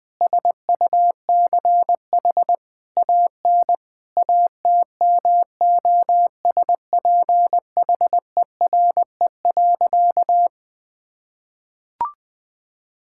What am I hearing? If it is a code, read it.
Morse code, heard as a keyed tone, about 20 wpm, 700 Hz. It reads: SUCH AN ATMOSPHERE.
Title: Sentences from Top 1600 Words 20wpm